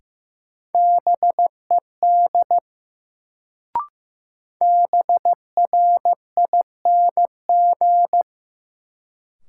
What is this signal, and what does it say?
Morse code, heard as a keyed tone, about 15 wpm, 700 Hz. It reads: BED BRING
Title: Sets of 5 Words Encore - New Word Drill - 15wpm